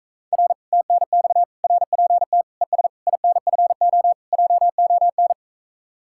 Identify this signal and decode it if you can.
Morse code, heard as a keyed tone, about 37 wpm, 700 Hz. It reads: R TNX RPT ES INFO JON